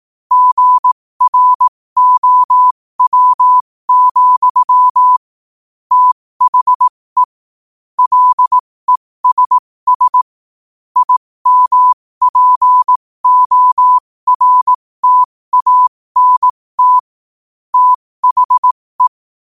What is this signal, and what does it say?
Morse code, heard as a keyed tone, about 18 wpm, spaced as for 15 wpm, 1000 Hz. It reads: GROW, THE LESS IMPORTANT THE
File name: qotdfiles / 15wpm